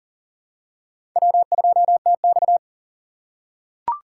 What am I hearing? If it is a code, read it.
Morse code, heard as a keyed tone, about 40 wpm, 700 Hz. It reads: W2TX
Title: USA Call Signs - 4 Characters - 4x Character Spacing - 40wpm